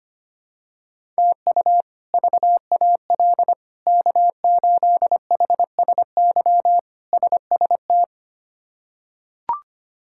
Morse code, heard as a keyed tone, about 25 wpm, 700 Hz. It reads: TU VAL K8HSY SST
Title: K1USN SST QSO Practice - 25wpm